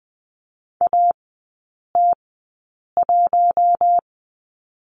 Morse code, heard as a keyed tone, about 20 wpm, spaced as for 8 wpm, 690 Hz. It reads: AT1
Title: 060 Lesson 11e - Letter h - Three Characters 20wpm